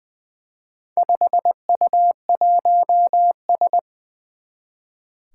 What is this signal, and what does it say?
Morse code, heard as a keyed tone, about 20 wpm, 700 Hz. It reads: 5U1S
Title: Four Challenging Letter-Number Combinations 20wpm